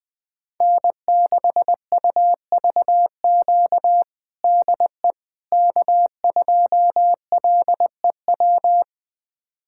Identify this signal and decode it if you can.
Morse code, heard as a keyed tone, about 21 wpm, 700 Hz. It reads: N6UVQ DE K2LEW
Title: USA Callsign DE Callsign - 5 Characters - 20wpm